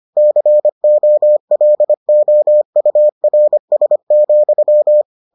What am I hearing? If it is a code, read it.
Morse code, heard as a keyed tone, about 25 wpm, 600 Hz. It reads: COLOURS,